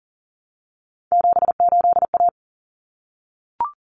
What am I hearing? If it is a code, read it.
Morse code, heard as a keyed tone, about 40 wpm, 700 Hz. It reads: 78A
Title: Common Cut Numbers - 3-Digit 40wpm